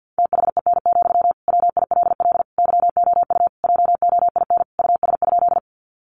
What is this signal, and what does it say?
Morse code, heard as a keyed tone, about 50 wpm, 700 Hz. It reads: THEN, WILL YOU JOIN US?
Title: Sentences from Top 1400 Words 50wpm